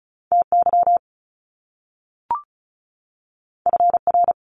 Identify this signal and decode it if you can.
Morse code, heard as a keyed tone, about 34 wpm, 700 Hz. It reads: TY FR
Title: Top Two Letter Combinations in Words 35wpm